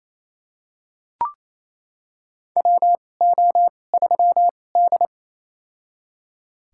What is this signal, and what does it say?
Morse code, heard as a keyed tone, about 28 wpm, spaced as for 20 wpm, 700 Hz. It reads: WO3D
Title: USA Call Signs - 4 Characters - 2x Character Spacing - 28wpm